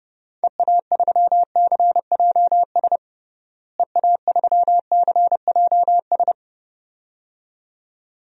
Morse code, heard as a keyed tone, about 30 wpm, 700 Hz. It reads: EA3CJS EA3CJS
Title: DXCC-100 Call Signs - 6 Characters - Repeated 2x - 30wpm